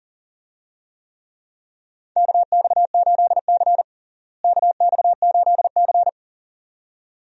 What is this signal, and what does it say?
Morse code, heard as a keyed tone, about 37 wpm, 700 Hz. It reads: KX8C KX8C
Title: USA Call Signs - 4 Characters - Repeated 2x - 40wpm